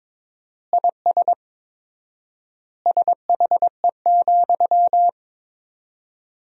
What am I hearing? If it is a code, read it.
Morse code, heard as a keyed tone, about 21 wpm, 700 Hz. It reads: IS SHE,
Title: 4x Word-Spacing - Sentences from Top 200 Words - 22wpm